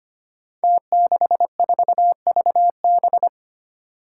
Morse code, heard as a keyed tone, about 25 wpm, 700 Hz. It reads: T64VB
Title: Five Challenging Letter-Number Combinations 25wpm